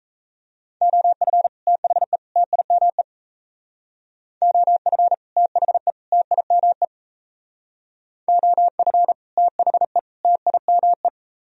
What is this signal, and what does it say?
Morse code, heard as a keyed tone, about 35 wpm, 700 Hz. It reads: OF THE TIME OF THE TIME OF THE TIME
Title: Speed-Racing - Sets of 3 Words 28wpm